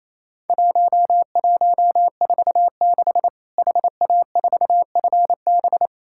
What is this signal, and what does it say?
Morse code, heard as a keyed tone, about 27 wpm, 700 Hz. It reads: 1146 HA4FB
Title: Worked All Europe DX Contest - QTC Exchange - 28wpm